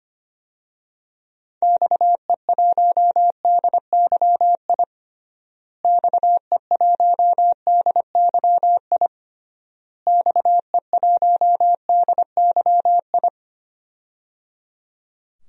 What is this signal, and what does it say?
Morse code, heard as a keyed tone, about 25 wpm, 700 Hz. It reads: XE1DYI XE1DYI XE1DYI
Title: DXCC-100 Call Signs - 6 Characters - Repeated 3x - 25wpm